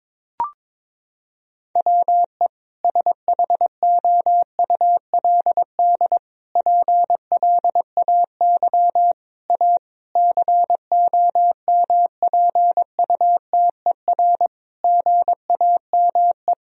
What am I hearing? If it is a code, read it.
Morse code, heard as a keyed tone, about 21 wpm, 700 Hz. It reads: WE SHOULD PLAY A COMPUTER GAME
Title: Sentences from Top 1000 Words - Review Part 3 - 22wpm